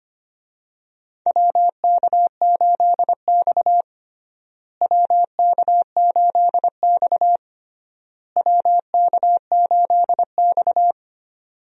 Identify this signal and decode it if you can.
Morse code, heard as a keyed tone, about 25 wpm, 700 Hz. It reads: WK8X WK8X WK8X
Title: USA Call Signs - 4 Characters - Repeated 3x - 25wpm